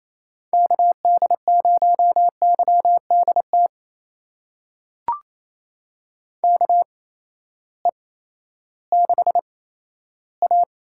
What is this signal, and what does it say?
Morse code, heard as a keyed tone, about 28 wpm, spaced as for 14 wpm, 700 Hz. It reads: KD0YDT KE6A
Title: USA Call Signs - 6 Characters - 8x Character Spacing - 28wpm